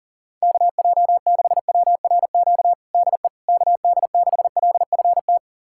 Morse code, heard as a keyed tone, about 40 wpm, 700 Hz. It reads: KJ6WRQ DE KD6LFT
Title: USA Callsign DE Callsign - 6 Characters - 40wpm